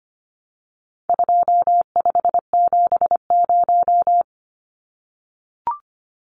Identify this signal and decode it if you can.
Morse code, heard as a keyed tone, about 25 wpm, 700 Hz. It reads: 2570
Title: Numbers - 4-Digit 25wpm